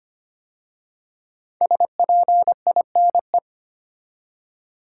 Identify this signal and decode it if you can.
Morse code, heard as a keyed tone, about 25 wpm, 700 Hz. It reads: SPINE
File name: 5 Letter Words - First 3 Letters Given - 25wpm